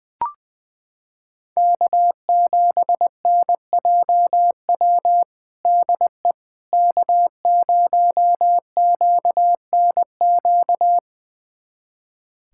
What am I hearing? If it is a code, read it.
Morse code, heard as a keyed tone, about 20 wpm, 700 Hz. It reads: K7NJW DE K0QNQ
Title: USA Callsign DE Callsign - 5 Characters - 20wpm